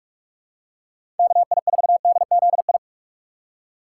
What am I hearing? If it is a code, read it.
Morse code, heard as a keyed tone, about 44 wpm, 700 Hz. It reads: KI4DZI
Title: USA Call Signs - 6 Characters - 6x Character Spacing - 45wpm